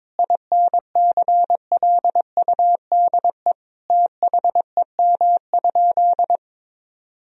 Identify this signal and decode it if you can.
Morse code, heard as a keyed tone, about 22 wpm, 700 Hz. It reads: INCLUDE THEM?
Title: Sentences from Top 1000 Words - Review Part 3 - 22wpm